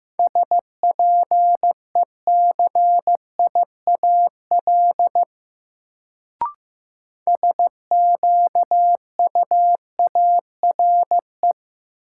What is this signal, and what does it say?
Morse code, heard as a keyed tone, about 15 wpm, 700 Hz. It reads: SPECIAL SQUARE
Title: Sentences from Top 1200 Words - New Word Drill Alt - 15wpm